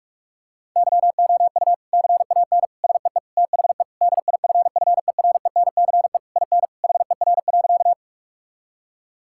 Morse code, heard as a keyed tone, about 46 wpm, 700 Hz. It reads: YOU CAN SEE THE DIFFERENCE IN HER.
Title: Sentences from Top 1000 Words - Review Part 5 - 45wpm